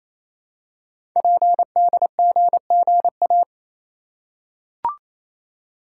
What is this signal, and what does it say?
Morse code, heard as a keyed tone, about 29 wpm, 700 Hz. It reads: PDGGA
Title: Code Groups - 5 Letters - 28wpm